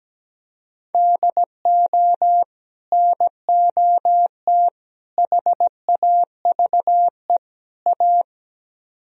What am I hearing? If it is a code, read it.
Morse code, heard as a keyed tone, about 17 wpm, 700 Hz. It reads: DO NOT HAVE A
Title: Speed-Racing - Sets of 4 Words 17wpm